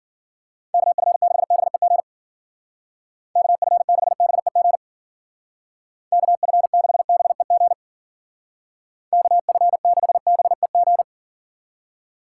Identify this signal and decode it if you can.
Morse code, heard as a keyed tone, about 52 wpm, 700 Hz. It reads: KF6BEG KF6BEG KF6BEG KF6BEG